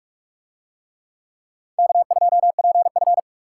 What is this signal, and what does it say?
Morse code, heard as a keyed tone, about 45 wpm, 700 Hz. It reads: K2PF